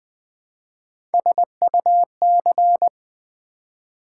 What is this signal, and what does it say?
Morse code, heard as a keyed tone, about 20 wpm, 690 Hz. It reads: SUC
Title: Build A Word - 1 Letter At A Time - 4-Letter Words - 20wpm